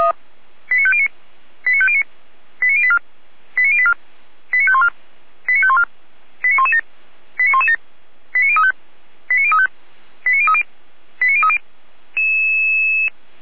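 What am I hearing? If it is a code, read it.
Five-tone selective call (ZVEI), heard as quick runs of five tones, tones of about 70 ms. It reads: 89509 89509 89084 89084 89514 89514 89117 89117 89035 89035 89030 89030
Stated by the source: Five-tone ZVEI selective call received on SDR.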